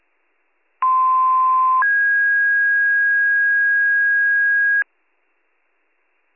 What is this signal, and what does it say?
Two-tone page, heard as a one-second tone, then a three-second tone, about 1040 then 1650 Hz. It is heard from 0.8 to 4.8 s.